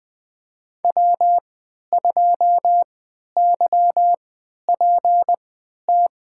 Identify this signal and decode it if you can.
Morse code, heard as a keyed tone, about 20 wpm, spaced as for 11 wpm, 700 Hz. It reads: W2YPT
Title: USA Call Signs - 5 Characters - 3x Character Spacing - 20wpm